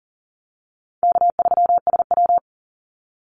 Morse code, heard as a keyed tone, about 41 wpm, 700 Hz. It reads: K3SW
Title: Call Signs - 40wpm